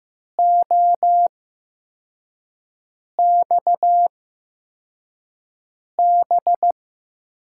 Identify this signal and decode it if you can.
Morse code, heard as a keyed tone, about 15 wpm, spaced as for 4 wpm, 700 Hz. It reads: OXB
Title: USA Call Signs - 6 Characters - 8x Character Spacing - 15wpm